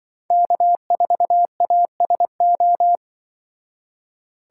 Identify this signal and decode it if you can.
Morse code, heard as a keyed tone, about 24 wpm, 700 Hz. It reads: K4ASO
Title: Speed Racing - USA 1x3 Call Signs 20wpm